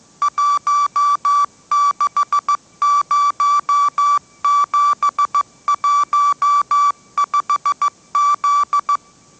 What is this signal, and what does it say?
Morse code, heard as a keyed tone, about 15 wpm, 1160 Hz. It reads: 160715Z